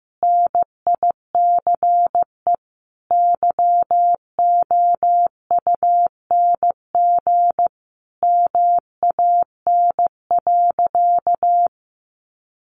Morse code, heard as a keyed tone, about 15 wpm, 700 Hz. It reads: NICE YOUNG MAN.
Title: Sentences from Top 1100 Words 15wpm